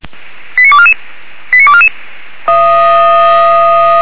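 Five-tone selective call (ZVEI), heard as quick runs of five tones, tones of about 70 ms. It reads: 89244 89244
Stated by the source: Five-tone ZVEI selective call received on an SDR, address 85729, channel 464.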